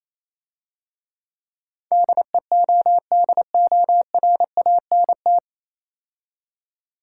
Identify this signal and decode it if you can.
Morse code, heard as a keyed tone, about 27 wpm, 700 Hz. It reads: DEODORANT
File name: Easy Words - Speed-Racing with Hints - 22wpm